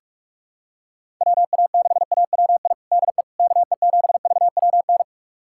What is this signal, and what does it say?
Morse code, heard as a keyed tone, about 43 wpm, 700 Hz. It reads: WA6AWI DE KE7VWN
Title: USA Callsign DE Callsign - 6 Characters - 45wpm